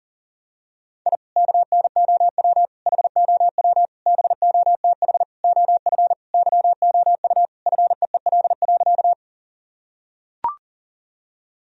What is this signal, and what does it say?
Morse code, heard as a keyed tone, about 40 wpm, 700 Hz. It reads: I KNOW HOW BOTH OF YOU FEEL.
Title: Sentences from Top 1000 Words - Review Part 1 - 40wpm